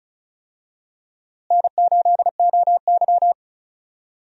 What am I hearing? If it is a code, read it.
Morse code, heard as a keyed tone, about 36 wpm, 700 Hz. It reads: N8OY
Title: USA Call Signs - 4 Characters - Repeated 2x - 35wpm